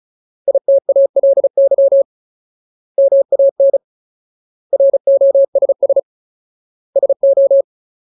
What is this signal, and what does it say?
Morse code, heard as a keyed tone, about 34 wpm, 550 Hz. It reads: ITALY MAN ROSS SO
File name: Ultimate Word List / 35WPM Ultimate